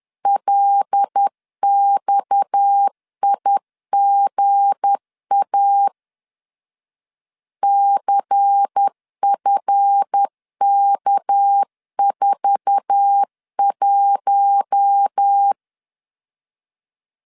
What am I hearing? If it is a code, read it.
Morse code, heard as a keyed tone, about 10 wpm, 800 Hz. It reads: LXIGA CFK41
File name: CW / DET / 1 9 1500C S A 10wpm 4511